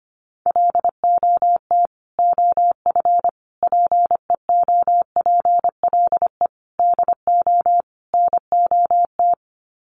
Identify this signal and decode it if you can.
Morse code, heard as a keyed tone, about 25 wpm, 690 Hz. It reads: LOT OF PEOPLE DO NOT
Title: Sets of 5 Words Encore 25wpm